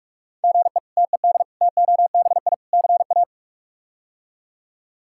Morse code, heard as a keyed tone, about 43 wpm, 700 Hz. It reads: GE TED TOBI CA
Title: K1USN SST QSO Practice - 45wpm